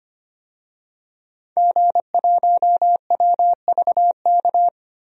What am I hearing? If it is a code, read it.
Morse code, heard as a keyed tone, about 25 wpm, 700 Hz. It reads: G1WVK
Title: DXCC-100 Call Signs - 5 Characters - 25wpm